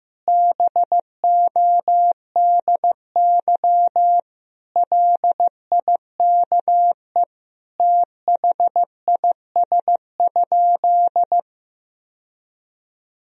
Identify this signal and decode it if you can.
Morse code, heard as a keyed tone, about 16 wpm, 700 Hz. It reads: BODY LIKE THIS?
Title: Sentences from Top 1000 Words - Review Part 2 - 15wpm